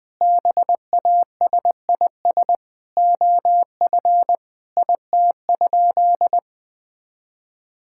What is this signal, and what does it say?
Morse code, heard as a keyed tone, about 21 wpm, 700 Hz. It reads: BASIS OF IT?
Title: Sentences from Top 1000 Words - Review Part 5 - 20wpm